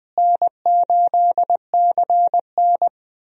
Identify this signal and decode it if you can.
Morse code, heard as a keyed tone, about 21 wpm, 700 Hz. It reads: N8CN
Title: USA Call Signs - 4 Characters - Repeated 2x - 20wpm